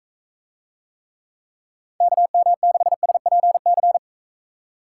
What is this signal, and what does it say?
Morse code, heard as a keyed tone, about 38 wpm, 700 Hz. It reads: KM6SPC